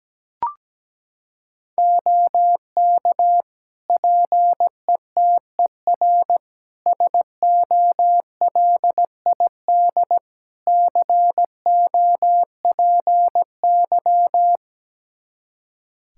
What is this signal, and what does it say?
Morse code, heard as a keyed tone, about 17 wpm, 700 Hz. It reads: OK PETER SOLID COPY
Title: K1USN SST QSO Practice 17wpm